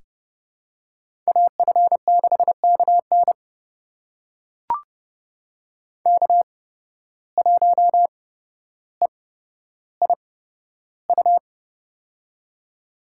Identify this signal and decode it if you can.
Morse code, heard as a keyed tone, about 30 wpm, spaced as for 13 wpm, 700 Hz. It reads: AF6KN K1EIU
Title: USA Call Signs - 5 Characters - 8x Character Spacing - 30wpm